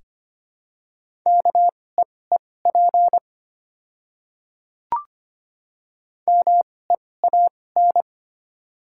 Morse code, heard as a keyed tone, about 25 wpm, spaced as for 18 wpm, 700 Hz. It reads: KEEP MEAN
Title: 2x Character Spacing - Top 300 Words - 25wpm